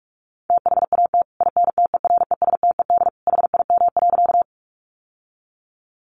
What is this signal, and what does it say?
Morse code, heard as a keyed tone, about 44 wpm, 700 Hz. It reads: THAT INTERESTED HIM.
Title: Sentences from Top 900 Words 45wpm